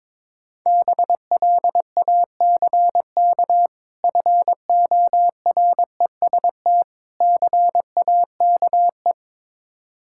Morse code, heard as a keyed tone, about 21 wpm, 690 Hz. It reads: BLACK FOREST CAKE
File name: Difficult Words with Hints - 22wpm